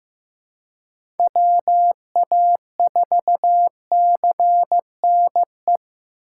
Morse code, heard as a keyed tone, about 15 wpm, 700 Hz. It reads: WA4CNE
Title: USA Call Signs - 6 Characters - 6x Character Spacing - 15wpm